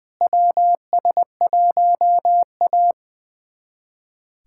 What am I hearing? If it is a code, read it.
Morse code, heard as a keyed tone, about 20 wpm, 700 Hz. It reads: WS1A
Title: USA Callsign DE Callsign - 4 Characters - 20wpm